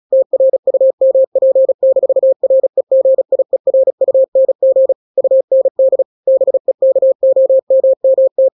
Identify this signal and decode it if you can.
Morse code, heard as a keyed tone, about 36 wpm, 530 Hz. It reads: TRUMP-REGIERUNG UND BEKOMMT
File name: daily / 35wpm